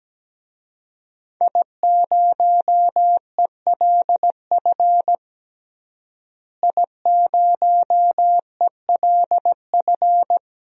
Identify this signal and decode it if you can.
Morse code, heard as a keyed tone, about 17 wpm, 700 Hz. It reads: I0ELF I0ELF
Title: DXCC-100 Call Signs - 5 Characters - Repeated 2x - 17wpm